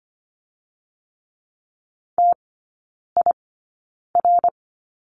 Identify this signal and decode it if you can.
Morse code, heard as a keyed tone, about 25 wpm, spaced as for 9 wpm, 700 Hz. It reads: TIR
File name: course / 25:10wpm / 093 Lesson 16e - Letter c - Three Characters 25wpm